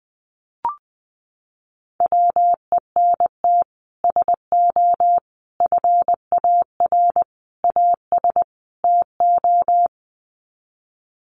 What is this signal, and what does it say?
Morse code, heard as a keyed tone, about 20 wpm, 700 Hz. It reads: WENT SO FAR AS TO
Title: Sets of 5 Words Encore 20wpm